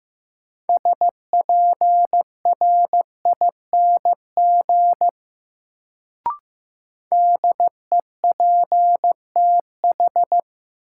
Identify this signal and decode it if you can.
Morse code, heard as a keyed tone, about 15 wpm, 700 Hz. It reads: SPRING DEPTH
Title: Sentences from Top 1200 Words - New Word Drill - 15wpm